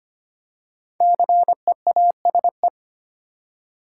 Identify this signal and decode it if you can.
Morse code, heard as a keyed tone, about 25 wpm, 700 Hz. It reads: CEASE
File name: 5 Letter Words - First 3 Letters Given - 25wpm